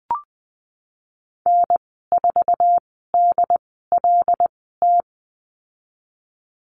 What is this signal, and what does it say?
Morse code, heard as a keyed tone, about 20 wpm, spaced as for 15 wpm, 700 Hz. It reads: N4DLT
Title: 088 Lesson 15g - Number 5 - Call Signs 20wpm